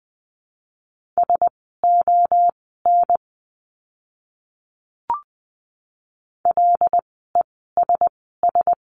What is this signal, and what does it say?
Morse code, heard as a keyed tone, about 20 wpm, spaced as for 14 wpm, 700 Hz. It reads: SON LESS
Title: 101 Lesson 17f - Letter u - Words 20wpm